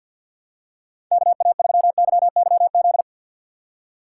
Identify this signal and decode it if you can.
Morse code, heard as a keyed tone, about 53 wpm, 700 Hz. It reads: KA3YYZ
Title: USA Call Signs - 6 Characters - 8x Character Spacing - 50wpm